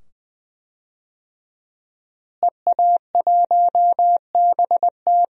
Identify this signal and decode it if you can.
Morse code, heard as a keyed tone, about 20 wpm, 700 Hz. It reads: EA1BT
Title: DXCC-100 Call Signs - 5 Characters - Repeated 2x - 20wpm